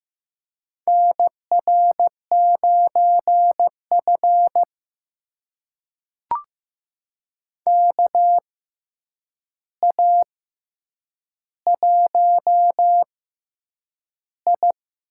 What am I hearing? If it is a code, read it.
Morse code, heard as a keyed tone, about 15 wpm, spaced as for 8 wpm, 700 Hz. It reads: NR9F KA1I
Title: USA Call Signs - 4 Characters - 6x Character Spacing - 15wpm